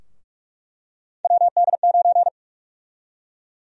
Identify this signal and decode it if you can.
Morse code, heard as a keyed tone, about 46 wpm, 700 Hz. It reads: WD9
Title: USA Call Sign Prefixes - Repeated 2x - 45wpm